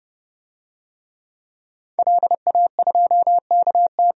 Morse code, heard as a keyed tone, about 30 wpm, 700 Hz. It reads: LA2KT
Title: DXCC-100 Call Signs - 5 Characters - 30wpm